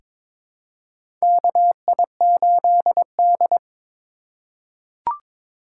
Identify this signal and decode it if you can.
Morse code, heard as a keyed tone, about 22 wpm, 700 Hz. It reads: KI8D
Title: USA Call Signs - 4 Characters - 3x Character Spacing - 22wpm